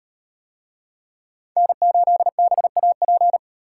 Morse code, heard as a keyed tone, about 38 wpm, 700 Hz. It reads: N8BAP